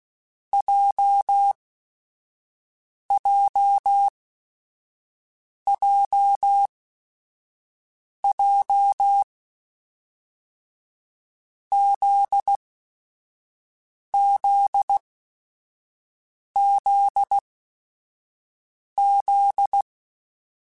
Morse code, heard as a keyed tone, about 16 wpm, spaced as for 5 wpm, 800 Hz. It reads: JJJJ ZZZZ
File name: Groups@5 / Groups@5 mp3s / JZ17@5